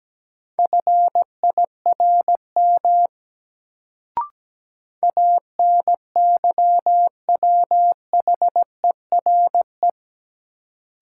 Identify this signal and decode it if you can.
Morse code, heard as a keyed tone, about 17 wpm, 700 Hz. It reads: FIRM ANYWHERE
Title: Sentences from Top 1400 Words - New Word Drill Alt - 17wpm